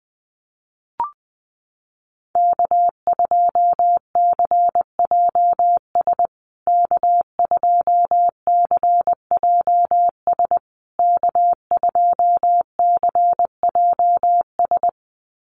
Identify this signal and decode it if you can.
Morse code, heard as a keyed tone, about 20 wpm, 700 Hz. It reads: K2CJS K2CJS K2CJS
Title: USA Call Signs Repeated 3x 20wpm